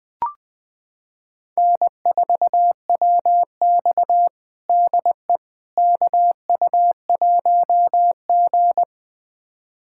Morse code, heard as a keyed tone, about 20 wpm, 700 Hz. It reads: N4WX DE KU1G
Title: USA Callsign DE Callsign - 4 Characters - 20wpm